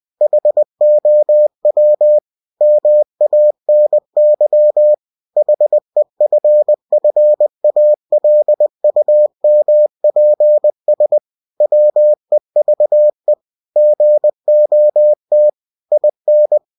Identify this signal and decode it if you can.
Morse code, heard as a keyed tone, about 20 wpm, 600 Hz. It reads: HOW MANY HEFFALUMPS WEVE GOT IN
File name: cw-books / winnie-the-pooh / winnie-20wpm-page-0038